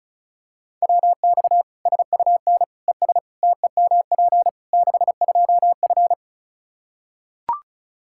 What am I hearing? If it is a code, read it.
Morse code, heard as a keyed tone, about 36 wpm, 700 Hz. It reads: WX SUN ES TEMP 62F
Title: Practice QSO - Exchange 2 - Info - 35wpm